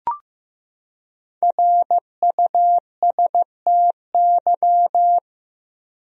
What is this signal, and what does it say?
Morse code, heard as a keyed tone, about 15 wpm, 700 Hz. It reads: RUSTY
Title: USA Common Names - Max 5 Letters - 15wpm